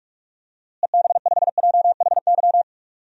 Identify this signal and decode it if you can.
Morse code, heard as a keyed tone, about 46 wpm, 700 Hz. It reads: EB5JHY
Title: DXCC-100 Call Signs - 6 Characters - Repeated 3x - 45wpm